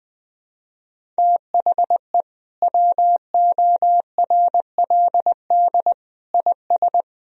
Morse code, heard as a keyed tone, about 20 wpm, 700 Hz. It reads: THE WORLD IS
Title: Sentences from Top 200 Words and Abbreviations 20wpm